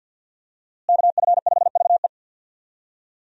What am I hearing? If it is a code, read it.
Morse code, heard as a keyed tone, about 50 wpm, 700 Hz. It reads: KF5VE
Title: USA Call Signs - 5 Characters - Repeated 2x - 50wpm